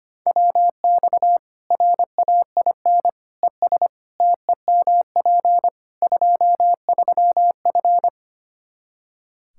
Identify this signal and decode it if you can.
Morse code, heard as a keyed tone, about 25 wpm, 700 Hz. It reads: WX RAIN ES TEMP 23F
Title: Practice QSO - Exchange 2 - Info - 25wpm